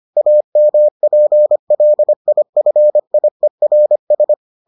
Morse code, heard as a keyed tone, about 25 wpm, 600 Hz. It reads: AMPLIFIERS